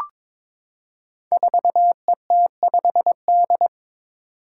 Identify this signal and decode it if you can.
Morse code, heard as a keyed tone, about 22 wpm, 700 Hz. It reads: 4ET5D
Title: Five Challenging Letter-Number Combinations 22wpm